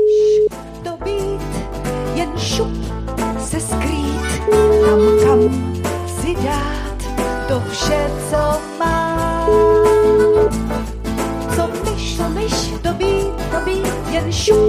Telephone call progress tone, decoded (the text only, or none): ringback tone